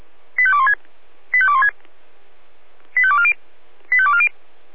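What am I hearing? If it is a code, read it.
Five-tone selective call (ZVEI), heard as quick runs of five tones, tones of about 70 ms. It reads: 85216 85216 85240 85240
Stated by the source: ZVEI five-tone selective call received on an SDR.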